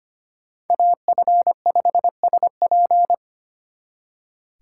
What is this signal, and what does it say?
Morse code, heard as a keyed tone, about 32 wpm, 700 Hz. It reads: AF5SP